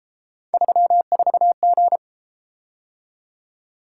Morse code, heard as a keyed tone, about 31 wpm, 700 Hz. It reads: 34G